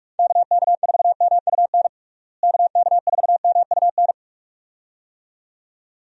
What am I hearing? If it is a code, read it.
Morse code, heard as a keyed tone, about 47 wpm, 700 Hz. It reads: KK4MUN KK4MUN
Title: USA Call Signs - 6 Characters - Repeated 2x - 45wpm